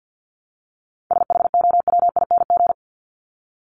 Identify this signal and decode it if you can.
Morse code, heard as a keyed tone, about 50 wpm, 700 Hz. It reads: SHOWING
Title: ING Suffix - Top 100 Words with Base Words 50wpm